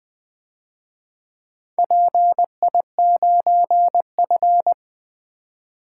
Morse code, heard as a keyed tone, about 24 wpm, 700 Hz. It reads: PI9F
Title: Speed-Racing - Four Letter-Number Combinations - 20wpm